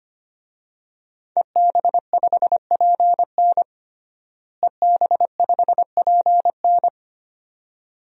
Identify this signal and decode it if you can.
Morse code, heard as a keyed tone, about 25 wpm, 700 Hz. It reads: EB5PN EB5PN
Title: DXCC-100 Call Signs - 5 Characters - Repeated 2x - 25wpm